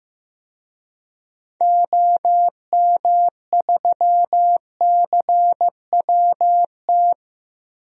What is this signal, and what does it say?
Morse code, heard as a keyed tone, about 15 wpm, 700 Hz. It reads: OM3CWT
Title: DXCC-100 Call Signs - 6 Characters - Repeated 2x - 15wpm